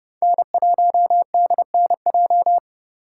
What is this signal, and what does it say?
Morse code, heard as a keyed tone, about 36 wpm, 700 Hz. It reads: N1DNJ
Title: Speed Racing - USA Call Signs 30wpm